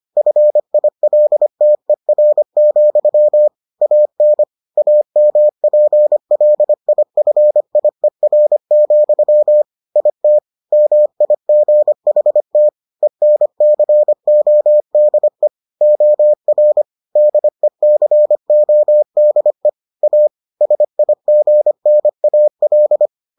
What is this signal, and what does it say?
Morse code, heard as a keyed tone, about 25 wpm, 600 Hz. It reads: FILTER, AN AMPLIFIER, IT MIGHT ENCODE OR DECODE A SIGNAL